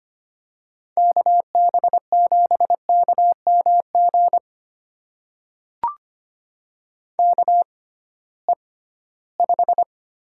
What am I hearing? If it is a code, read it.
Morse code, heard as a keyed tone, about 25 wpm, spaced as for 16 wpm, 700 Hz. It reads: KB7KMG KE5
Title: USA Call Signs - 6 Characters - 6x Character Spacing - 25wpm